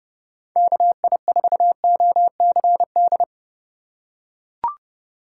Morse code, heard as a keyed tone, about 30 wpm, 700 Hz. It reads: KI4OCD
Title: USA Call Signs - 3x Character Spacing - 30wpm